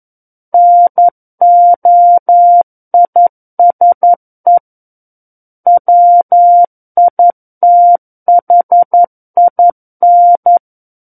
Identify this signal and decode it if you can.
Morse code, heard as a keyed tone, about 11 wpm, 700 Hz. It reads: NOISE WITHIN
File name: Book One / 02-Chapter 2 - 11 WPM